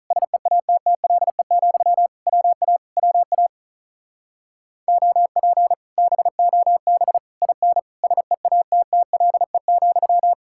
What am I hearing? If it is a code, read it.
Morse code, heard as a keyed tone, about 45 wpm, 700 Hz. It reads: SEATTLE, WA WA OP BOB IN SEATTLE,